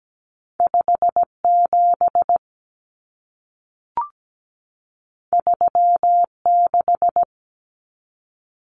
Morse code, heard as a keyed tone, about 17 wpm, 700 Hz. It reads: 5 7 3 6
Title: Common Cut Numbers - 2-Digit 17wpm